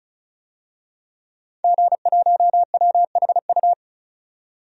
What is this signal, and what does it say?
Morse code, heard as a keyed tone, about 36 wpm, 700 Hz. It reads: G1WHU
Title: DXCC-100 Call Signs - 5 Characters - 35wpm